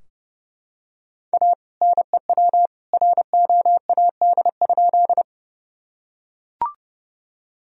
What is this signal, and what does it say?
Morse code, heard as a keyed tone, about 30 wpm, 700 Hz. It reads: A NEW ROAD?
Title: Sentences from Top 1000 Words - Review Part 2 - 30wpm